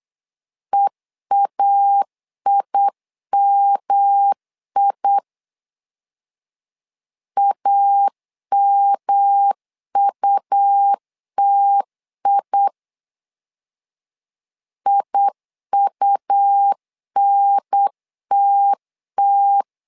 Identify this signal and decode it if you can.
Morse code, heard as a keyed tone, about 8 wpm, 800 Hz. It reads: EAIMI AMUTI IUNTT
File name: LEZIONI / 3 MWRO / 3 M 8wpm 2300